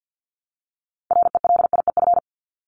Morse code, heard as a keyed tone, about 50 wpm, 700 Hz. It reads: RELIEF
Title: Top 1501-1750 Words 50wpm